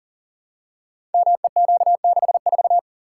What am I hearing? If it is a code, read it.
Morse code, heard as a keyed tone, about 39 wpm, 700 Hz. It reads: MEQ64
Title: Code Groups - 5 Letter:Numbers - 40wpm